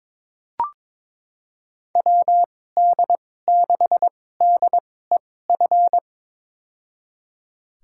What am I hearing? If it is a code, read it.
Morse code, heard as a keyed tone, about 21 wpm, spaced as for 16 wpm, 700 Hz. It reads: WD6DEF
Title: USA Call Signs - 6 Characters - 2x Character Spacing - 22wpm